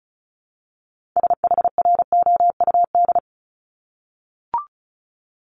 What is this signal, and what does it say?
Morse code, heard as a keyed tone, about 36 wpm, 700 Hz. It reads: SHROUD
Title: Multi-letter Phonemes Repeated 3x 35wpm